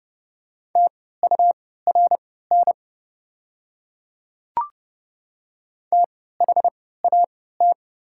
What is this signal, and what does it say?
Morse code, heard as a keyed tone, about 30 wpm, spaced as for 17 wpm, 700 Hz. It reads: TURN THAT
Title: 3x Character Spacing - Top 500 Words - 30wpm